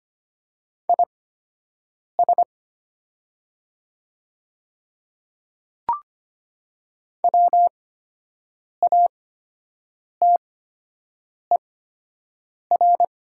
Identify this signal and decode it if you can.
Morse code, heard as a keyed tone, about 25 wpm, spaced as for 7 wpm, 700 Hz. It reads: IS WATER
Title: 8x Character Spacing - Top 300 Words - 25wpm